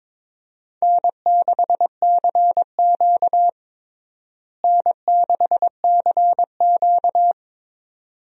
Morse code, heard as a keyed tone, about 22 wpm, 700 Hz. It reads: N6CQ N6CQ
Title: USA Call Signs - 4 Characters - Repeated 2x - 22wpm